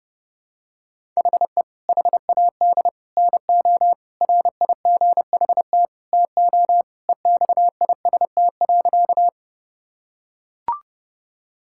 Morse code, heard as a keyed tone, about 32 wpm, 700 Hz. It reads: HE HAD NO RIGHT TO EXIST.
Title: Sentences from Top 1000 Words - Review Part 5 - 30wpm